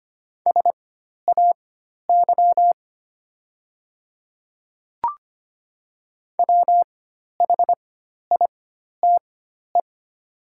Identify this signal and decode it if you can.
Morse code, heard as a keyed tone, about 25 wpm, spaced as for 12 wpm, 700 Hz. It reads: SAY WHITE
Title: 4x Character Spacing - Top 500 Words - 25wpm